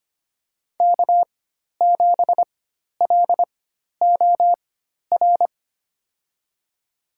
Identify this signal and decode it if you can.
Morse code, heard as a keyed tone, about 25 wpm, spaced as for 12 wpm, 700 Hz. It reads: K7LOR
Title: USA Call Signs - 5 Characters - 4x Character Spacing - 25wpm